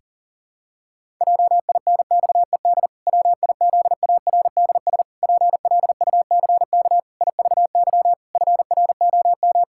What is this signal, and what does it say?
Morse code, heard as a keyed tone, about 40 wpm, 700 Hz. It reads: JINXED WIZARDS PLUCK IVY FROM
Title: Pangram Challenge v2 - 40wpm